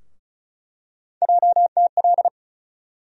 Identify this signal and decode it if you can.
Morse code, heard as a keyed tone, about 36 wpm, 700 Hz. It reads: JTL
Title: USA Call Sign Postfixes - 3 Letters 35wpm